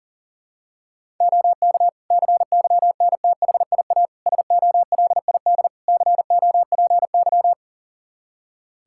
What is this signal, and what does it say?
Morse code, heard as a keyed tone, about 40 wpm, 700 Hz. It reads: OK CYNTHIA SOLID COPY
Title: K1USN SST QSO Practice 40wpm